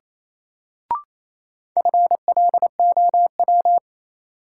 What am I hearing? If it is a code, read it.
Morse code, heard as a keyed tone, about 27 wpm, 700 Hz. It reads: FLOW
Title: Sentences from Top 1200 Words - New Word Drill Alt - 28wpm